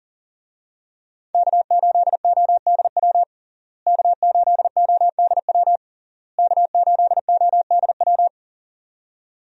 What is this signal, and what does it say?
Morse code, heard as a keyed tone, about 40 wpm, 700 Hz. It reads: K8ODW K8ODW K8ODW
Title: USA Call Signs - 5 Characters - Repeated 3x - 40wpm